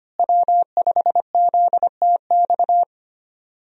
Morse code, heard as a keyed tone, about 25 wpm, 700 Hz. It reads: W5ZTX
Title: USA Call Signs - 5 Characters - Repeated 2x - 25wpm